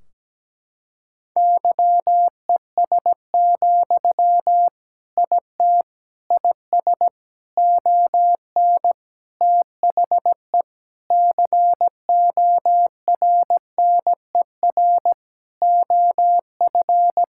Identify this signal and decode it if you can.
Morse code, heard as a keyed tone, about 17 wpm, 700 Hz. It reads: YES, IT IS ON THE CORNER OF
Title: Sentences from Top 1600 Words 17wpm